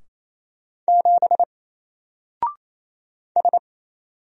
Morse code, heard as a keyed tone, about 28 wpm, 700 Hz. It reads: 7 S
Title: Single Character (Letter:Number:Punctuation:Prosign) 28wpm